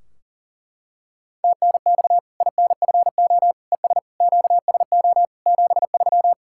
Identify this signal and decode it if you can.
Morse code, heard as a keyed tone, about 40 wpm, 700 Hz. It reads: TNX INFO ES QSO 73
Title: Practice QSO - Exchange 4 - Ending - 40wpm